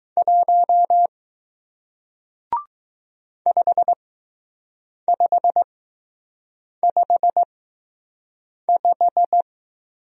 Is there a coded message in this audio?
Yes: Morse code from 0.2 to 9.4 s, heard as a keyed tone, about 20 wpm, spaced as for 6 wpm, 700 Hz. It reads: 1 5555